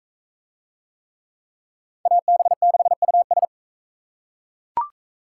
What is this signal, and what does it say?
Morse code, heard as a keyed tone, about 39 wpm, 700 Hz. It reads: AB6US